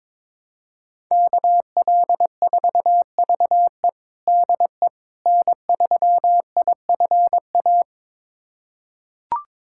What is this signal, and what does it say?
Morse code, heard as a keyed tone, about 22 wpm, 700 Hz. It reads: KL4VE DE N3IFA
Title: USA Callsign DE Callsign - 5 Characters - 22wpm